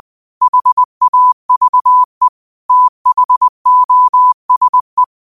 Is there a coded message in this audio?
Yes: Morse code from 0.4 to 5.0 s, heard as a keyed tone, about 20 wpm, 1000 Hz. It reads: HAVE THOSE